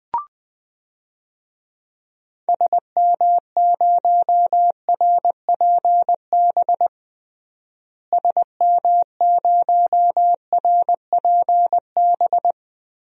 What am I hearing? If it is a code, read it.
Morse code, heard as a keyed tone, about 20 wpm, 700 Hz. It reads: SM0RPB SM0RPB
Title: DXCC-100 Call Signs - 6 Characters - Repeated 3x - 20wpm